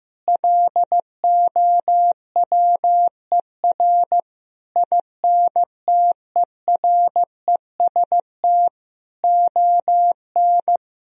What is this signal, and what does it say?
Morse code, heard as a keyed tone, about 16 wpm, 700 Hz. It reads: LOWER INTEREST ON
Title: Sentences from Top 1000 Words - Review Part 4 - 15wpm